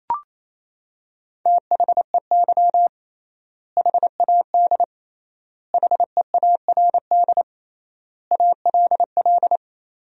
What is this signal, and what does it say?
Morse code, heard as a keyed tone, about 28 wpm, 700 Hz. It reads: THEY HAD HEARD ALL
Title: 3x Word-Spacing - Sentences from Top 400 Words - 28wpm